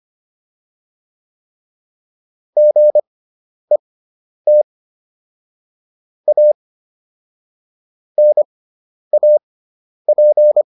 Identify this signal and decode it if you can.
Morse code, heard as a keyed tone, about 25 wpm, spaced as for 10 wpm, 600 Hz. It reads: GET A NAP